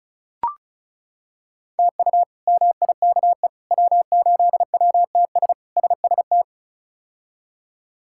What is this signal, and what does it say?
Morse code, heard as a keyed tone, about 36 wpm, 700 Hz. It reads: TU MIKE W8WTS SST
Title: K1USN SST QSO Practice - 35wpm